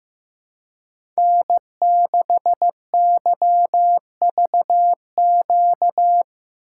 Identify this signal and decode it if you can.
Morse code, heard as a keyed tone, about 15 wpm, 700 Hz. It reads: N6YVQ
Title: USA Call Signs - 5 Characters - 4x Character Spacing - 15wpm